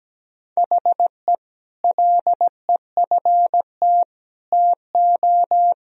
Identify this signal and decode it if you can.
Morse code, heard as a keyed tone, about 17 wpm, 700 Hz. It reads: HE LEFT TO
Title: Sentences from Top 1000 Words - Review Part 2 - 17wpm